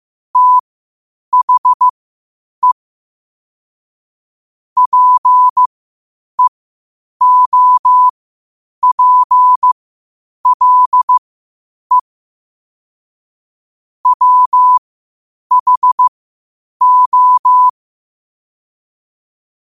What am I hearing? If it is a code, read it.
Morse code, heard as a keyed tone, about 15 wpm, spaced as for 8 wpm, 1000 Hz. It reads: THE PEOPLE WHO
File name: qotdfiles / 8wpm